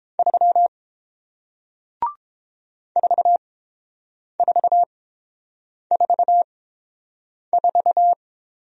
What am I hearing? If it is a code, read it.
Morse code, heard as a keyed tone, about 30 wpm, spaced as for 7 wpm, 690 Hz. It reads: 3 4444